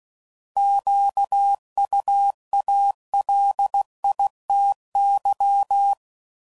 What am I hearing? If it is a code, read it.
Morse code, heard as a keyed tone, about 16 wpm, 800 Hz. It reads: QUALITY